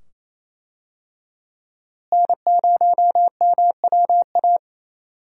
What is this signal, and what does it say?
Morse code, heard as a keyed tone, about 28 wpm, 700 Hz. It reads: N0MWA
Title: USA Call Signs - 5 Characters - Repeated 2x - 28wpm